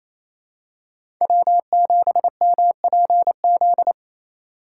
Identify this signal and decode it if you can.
Morse code, heard as a keyed tone, about 28 wpm, 700 Hz. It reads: W7MPZ
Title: USA Call Signs - 5 Characters - 4x Character Spacing - 28wpm